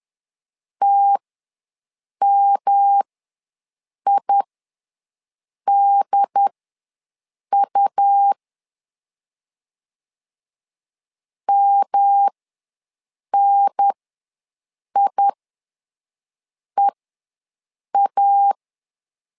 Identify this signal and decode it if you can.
Morse code, heard as a keyed tone, about 10 wpm, spaced as for 6 wpm, 800 Hz. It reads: TMIDU MNIEA
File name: LEZIONI / 3 MWRO / 3 M ws 10wpm 2121